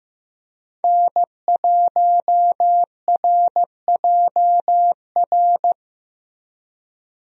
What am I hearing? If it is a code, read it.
Morse code, heard as a keyed tone, about 15 wpm, 700 Hz. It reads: N1RJR
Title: USA Call Signs - 5 Characters - Repeated 3x - 15wpm